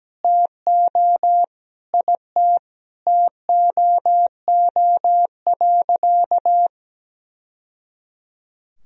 Morse code, heard as a keyed tone, about 17 wpm, 700 Hz. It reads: TO IT TOO.
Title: Sentences from Top 1000 Words - Review Part 1 - 17wpm